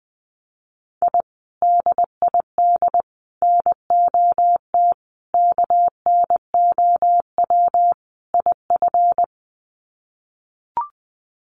Morse code, heard as a keyed tone, about 20 wpm, 700 Hz. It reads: I DID NOT KNOW IF
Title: Sets of 5 Words Encore 20wpm